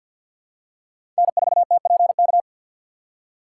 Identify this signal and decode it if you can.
Morse code, heard as a keyed tone, about 50 wpm, 690 Hz. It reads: N4TPK
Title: USA Call Signs - 2x Character Spacing - 50wpm